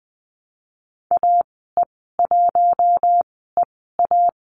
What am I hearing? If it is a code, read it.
Morse code, heard as a keyed tone, about 20 wpm, spaced as for 15 wpm, 700 Hz. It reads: AE1EA
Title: 042 Lesson 8g - Number 1 - Call Signs 20wpm